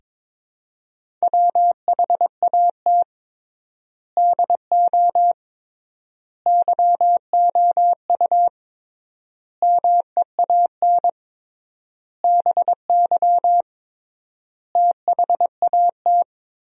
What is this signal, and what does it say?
Morse code, heard as a keyed tone, about 22 wpm, 700 Hz. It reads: WHAT DO YOU MEAN BY THAT
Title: 3x Word-Spacing - Sentences from Top 400 Words - 22wpm